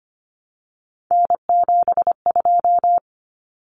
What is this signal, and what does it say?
Morse code, heard as a keyed tone, about 25 wpm, 700 Hz. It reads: N72
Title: Common Cut Numbers - 3-Digit 25wpm